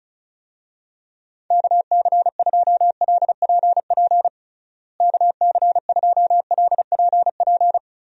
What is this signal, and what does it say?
Morse code, heard as a keyed tone, about 36 wpm, 700 Hz. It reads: KC2LPP KC2LPP
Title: USA Call Signs - 6 Characters - Repeated 3x - 35wpm